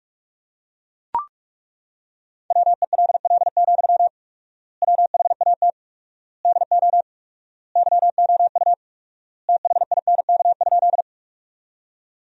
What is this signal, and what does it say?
Morse code, heard as a keyed tone, about 42 wpm, 700 Hz. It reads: WELL, WHAT DO YOU THINK?
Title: 4x Word-Spacing - Sentences from Top 200 Words - 45wpm